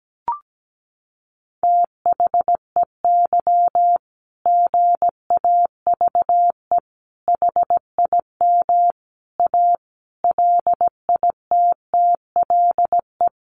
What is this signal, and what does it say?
Morse code, heard as a keyed tone, about 17 wpm, 700 Hz. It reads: THEY GAVE HIM A LITTLE
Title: Sentences from Top 1500 Words 17wpm